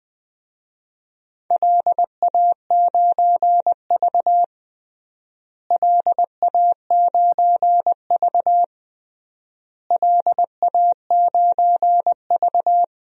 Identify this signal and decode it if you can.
Morse code, heard as a keyed tone, about 20 wpm, 700 Hz. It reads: LA9V LA9V LA9V
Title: DXCC-100 Call Signs - 4 Characters - Repeated 3x - 20wpm